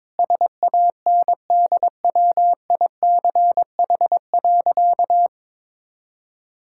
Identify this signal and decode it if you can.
Morse code, heard as a keyed tone, about 22 wpm, 690 Hz. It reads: SANDWICH.
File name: practice / 22wpm / Easy Sentences with Hints - 22wpm